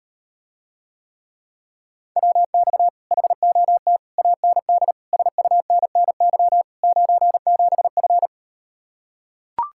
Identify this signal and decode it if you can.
Morse code, heard as a keyed tone, about 38 wpm, 700 Hz. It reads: WX HOT AND SUNNY 97F